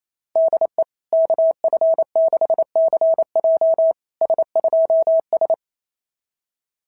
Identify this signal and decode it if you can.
Morse code, heard as a keyed tone, about 28 wpm, 660 Hz. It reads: DE KF6CJ S2S
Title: SOTA QSO Practice 28wpm